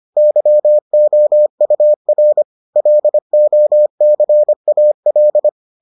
Morse code, heard as a keyed tone, about 25 wpm, 600 Hz. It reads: YOUR LOCAL